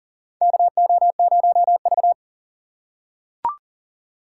Morse code, heard as a keyed tone, about 38 wpm, 700 Hz. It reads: KO0V
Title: USA Call Signs - 4 Characters - 3x Character Spacing - 40wpm